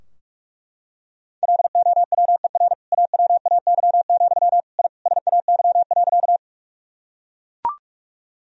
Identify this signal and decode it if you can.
Morse code, heard as a keyed tone, about 42 wpm, 700 Hz. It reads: LOWER AWAY, I SAY.
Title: Sentences from Top 1000 Words - Review Part 4 - 45wpm